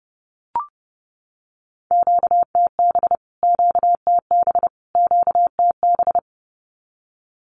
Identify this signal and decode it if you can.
Morse code, heard as a keyed tone, about 30 wpm, 700 Hz. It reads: QTB QTB QTB
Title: Q-Codes Repeated 3x 30wpm